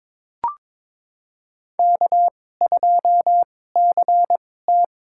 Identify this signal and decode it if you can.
Morse code, heard as a keyed tone, about 22 wpm, spaced as for 16 wpm, 700 Hz. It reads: K2CT
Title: USA Call Signs - 4 Characters - 2x Character Spacing - 22wpm